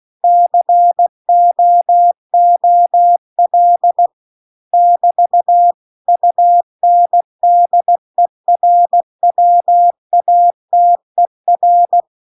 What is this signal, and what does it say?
Morse code, heard as a keyed tone, about 16 wpm, 700 Hz. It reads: COOL <BT> UNDERWATER